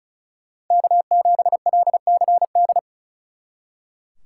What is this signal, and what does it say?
Morse code, heard as a keyed tone, about 47 wpm, spaced as for 41 wpm, 700 Hz. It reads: K7LCD
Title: Speed Racing - USA 1x3 Call Signs 35wpm